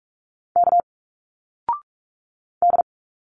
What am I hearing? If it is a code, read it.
Morse code, heard as a keyed tone, about 42 wpm, 700 Hz. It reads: K D
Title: Single Letters 45wpm